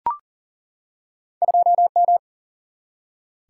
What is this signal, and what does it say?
Morse code, heard as a keyed tone, about 40 wpm, 700 Hz. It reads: 2M
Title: Single-Letter Single-Number 40wpm